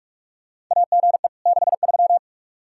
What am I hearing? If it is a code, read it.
Morse code, heard as a keyed tone, about 46 wpm, 700 Hz. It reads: AGE 63
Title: Practice QSO - Exchange 3 - More Info - 45wpm